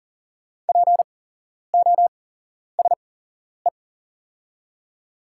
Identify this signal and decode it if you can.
Morse code, heard as a keyed tone, about 38 wpm, spaced as for 11 wpm, 700 Hz. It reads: POSE
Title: 8x Character Spacing - Top 500 Words - 40wpm